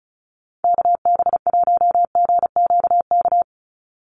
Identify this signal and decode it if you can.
Morse code, heard as a keyed tone, about 34 wpm, 700 Hz. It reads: KB1GQK
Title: USA Call Signs 35wpm